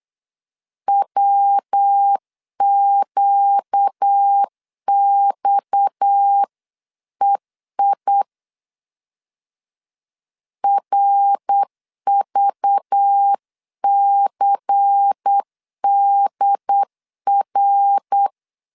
Morse code, heard as a keyed tone, about 8 wpm, 800 Hz. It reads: WQXEI RVCDR
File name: CW / DET / 1 6 1500C S B 8wpm 5142